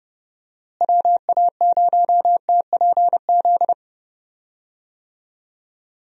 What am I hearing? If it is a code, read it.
Morse code, heard as a keyed tone, about 30 wpm, 700 Hz. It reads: WA0TPZ
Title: USA Call Signs - 6 Characters - Repeated 2x - 30wpm